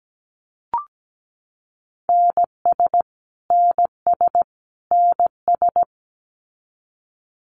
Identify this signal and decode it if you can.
Morse code, heard as a keyed tone, about 17 wpm, 700 Hz. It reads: NS NS NS
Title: Canadian Provinces and Territories - Repeated 3x - 17wpm